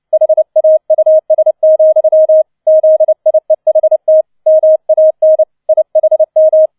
Morse code, heard as a keyed tone, about 29 wpm, 630 Hz. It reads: HAUS, ZIEHT MAN IHM